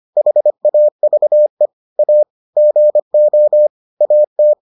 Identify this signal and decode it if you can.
Morse code, heard as a keyed tone, about 25 wpm, 600 Hz. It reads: HAVE A GO AT